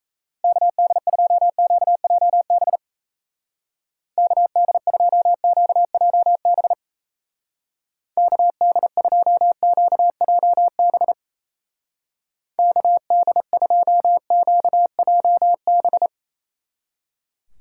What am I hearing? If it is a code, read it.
Morse code, heard as a keyed tone, about 37 wpm, 700 Hz. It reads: KD2QJB KD2QJB KD2QJB KD2QJB